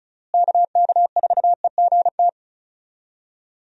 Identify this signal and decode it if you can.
Morse code, heard as a keyed tone, about 34 wpm, 700 Hz. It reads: KK4EGT
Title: USA Call Signs - 6 Characters - Repeated 3x - 35wpm